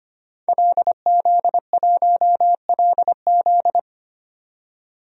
Morse code, heard as a keyed tone, about 25 wpm, 700 Hz. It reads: LZ1LZ
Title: DXCC-100 Call Signs - 5 Characters - 25wpm